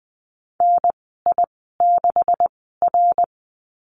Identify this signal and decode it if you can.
Morse code, heard as a keyed tone, about 20 wpm, spaced as for 15 wpm, 700 Hz. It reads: NI6R
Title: 128 Lesson 21g - Number 6 - Call Signs 20wpm